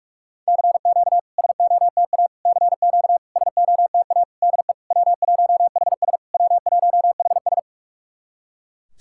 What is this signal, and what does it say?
Morse code, heard as a keyed tone, about 46 wpm, 700 Hz. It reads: CQ SOTA CQ SOTA DE W1HS W1HS
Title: SOTA QSO Practice 45wpm